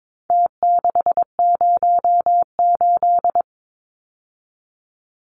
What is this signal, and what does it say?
Morse code, heard as a keyed tone, about 22 wpm, 700 Hz. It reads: T608
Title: Numbers - Contest Style - 4-Digit 22wpm